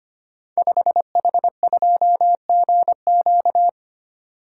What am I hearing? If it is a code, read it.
Morse code, heard as a keyed tone, about 25 wpm, 700 Hz. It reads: 5H2GQ
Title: Code Groups - 5 Letter:Numbers - 25wpm